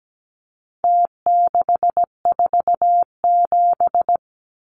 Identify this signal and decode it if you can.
Morse code, heard as a keyed tone, about 17 wpm, 700 Hz. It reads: T647
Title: Numbers - Contest Style - 4-Digit 17wpm